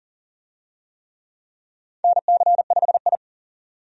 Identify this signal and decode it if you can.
Morse code, heard as a keyed tone, about 40 wpm, 700 Hz. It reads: NC5I
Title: USA Call Signs - 4 Characters - 40wpm